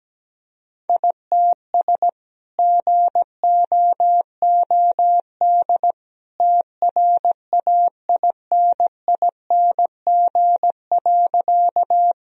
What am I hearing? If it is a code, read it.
Morse code, heard as a keyed tone, about 17 wpm, 700 Hz. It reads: ITS GOOD TRAINING.
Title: Sentences from Top 1000 Words - Review Part 4 - 17wpm